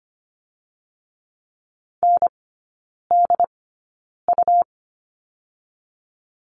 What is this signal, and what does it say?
Morse code, heard as a keyed tone, about 25 wpm, 700 Hz. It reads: N D U
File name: Lesson 17e - Letter u - Three Characters 25wpm